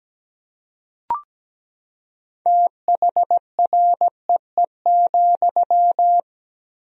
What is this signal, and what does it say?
Morse code, heard as a keyed tone, about 17 wpm, 700 Hz. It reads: THREE,
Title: 3x Word-Spacing - Sentences from Top 200 Words - 17wpm